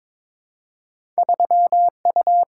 Morse code, heard as a keyed tone, about 22 wpm, 690 Hz. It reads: 3U
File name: practice / 22wpm / Two Letter-Number Combinations - 22wpm